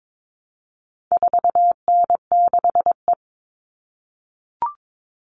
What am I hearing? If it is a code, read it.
Morse code, heard as a keyed tone, about 22 wpm, 700 Hz. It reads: 4N6E
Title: All Cut Numbers - 4-Digit 22wpm